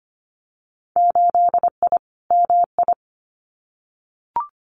Morse code, heard as a keyed tone, about 25 wpm, 700 Hz. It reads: 8I MI
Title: Winter Field Day Exchanges Encore 25wpm